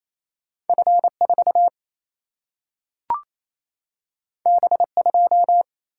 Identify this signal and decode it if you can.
Morse code, heard as a keyed tone, about 28 wpm, 700 Hz. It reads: F4 B2
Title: Single-Letter Single-Number 28wpm